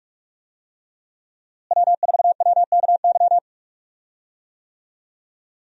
Morse code, heard as a keyed tone, about 43 wpm, 700 Hz. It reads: W4WKY
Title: USA Call Signs - 5 Characters - Repeated 2x - 45wpm